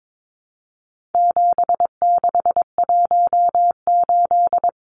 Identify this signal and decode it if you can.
Morse code, heard as a keyed tone, about 22 wpm, 700 Hz. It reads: 7618
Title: Numbers - 4-Digit 22wpm